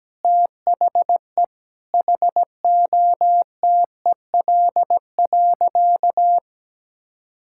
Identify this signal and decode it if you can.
Morse code, heard as a keyed tone, about 17 wpm, 700 Hz. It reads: THE HOTEL.
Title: Sentences from Top 1600 Words 17wpm